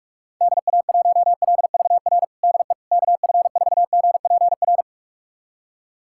Morse code, heard as a keyed tone, about 41 wpm, 690 Hz. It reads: DA1LVR DE KF4GPR
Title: USA Callsign DE Callsign - 6 Characters - 45wpm